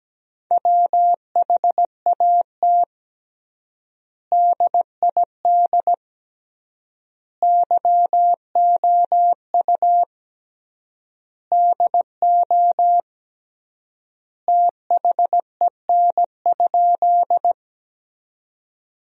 Morse code, heard as a keyed tone, about 17 wpm, 700 Hz. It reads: WHAT DID YOU DO THEN?
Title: 3x Word-Spacing - Sentences from Top 100 Words - 17wpm